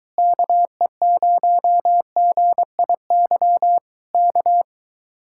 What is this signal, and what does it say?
Morse code, heard as a keyed tone, about 22 wpm, 700 Hz. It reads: KE0GIY K